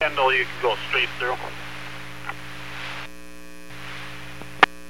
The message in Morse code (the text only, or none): none